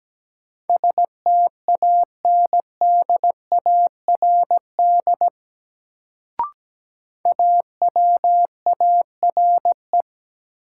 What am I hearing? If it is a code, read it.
Morse code, heard as a keyed tone, about 17 wpm, 700 Hz. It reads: STANDARD AWARE
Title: Sentences from Top 1200 Words - New Word Drill Alt - 17wpm